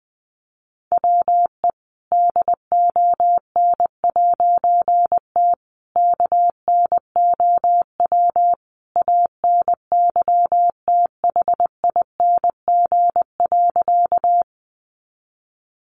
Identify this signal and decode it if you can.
Morse code, heard as a keyed tone, about 20 wpm, 700 Hz. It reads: WE DON'T KNOW ANYTHING.
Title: A Sound Of Thunder 20wpm